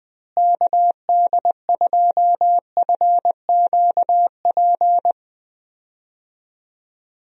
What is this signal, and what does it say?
Morse code, heard as a keyed tone, about 21 wpm, 700 Hz. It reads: KD2FQP
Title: USA Call Signs - 6 Characters - Repeated 3x - 20wpm